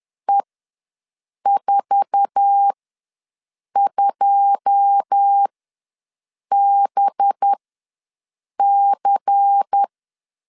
Morse code, heard as a keyed tone, about 10 wpm, spaced as for 6 wpm, 800 Hz. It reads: E42BC
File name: LEZIONI / 10 SIGNS / 10B INT ws 10wpm 4134